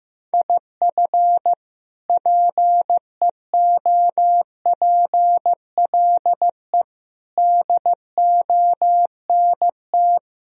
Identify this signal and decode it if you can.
Morse code, heard as a keyed tone, about 15 wpm, 700 Hz. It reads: IF PEOPLE DONT
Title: Sentences from Top 1000 Words - Review Part 3 - 15wpm